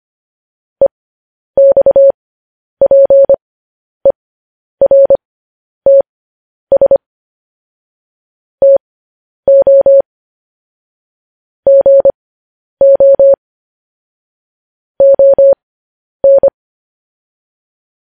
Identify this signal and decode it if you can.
Morse code, heard as a keyed tone, about 25 wpm, spaced as for 10 wpm, 550 Hz. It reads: EXPERTS TO GO ON